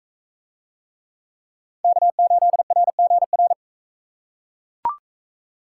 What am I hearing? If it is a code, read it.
Morse code, heard as a keyed tone, about 43 wpm, 700 Hz. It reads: K8RGR